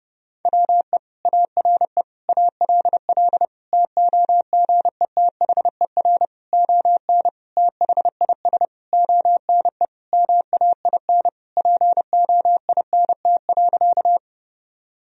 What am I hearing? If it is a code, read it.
Morse code, heard as a keyed tone, about 30 wpm, 700 Hz. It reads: WE ARE ALL TOGETHER ON THIS ONE MAIN POINT.
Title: Sentences from Top 1000 Words - Review Part 2 - 30wpm